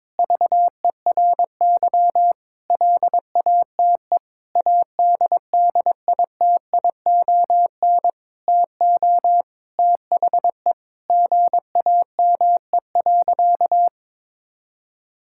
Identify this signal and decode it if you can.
Morse code, heard as a keyed tone, about 22 wpm, 700 Hz. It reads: VERY LATE ADDITION TO THE GAME.
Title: Sentences from Top 1000 Words - Review Part 6 - 22wpm